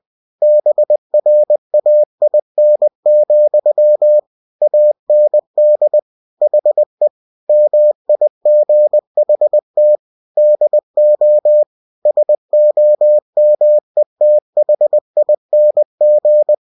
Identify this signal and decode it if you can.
Morse code, heard as a keyed tone, about 20 wpm, 600 Hz. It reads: BRAIN, AND HE MIGHT DO SOMETHING
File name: cw-books / winnie-the-pooh / winnie-20wpm-page-0097